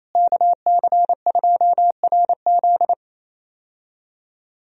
Morse code, heard as a keyed tone, about 28 wpm, 700 Hz. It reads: KC2RZ
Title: USA Call Signs - 5 Characters - Repeated 2x - 28wpm